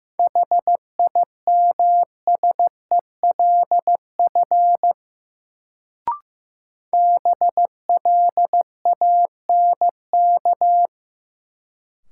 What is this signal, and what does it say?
Morse code, heard as a keyed tone, about 15 wpm, 700 Hz. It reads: HIMSELF BLANK
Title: Sentences from Top 1500 Words - New Word Drill - 15wpm